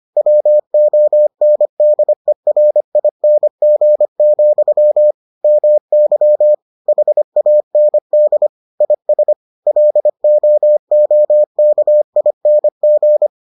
Morse code, heard as a keyed tone, about 25 wpm, 600 Hz. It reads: WONDERING, MY HAND IS LOOKING